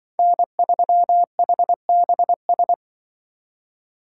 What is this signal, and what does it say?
Morse code, heard as a keyed tone, about 24 wpm, 700 Hz. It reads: N3HBS
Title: Speed Racing - USA 1x3 Call Signs 20wpm